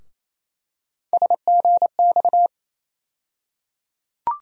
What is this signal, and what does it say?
Morse code, heard as a keyed tone, about 28 wpm, 700 Hz. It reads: SGX
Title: USA Call Sign Postfixes - 3 Letters 28wpm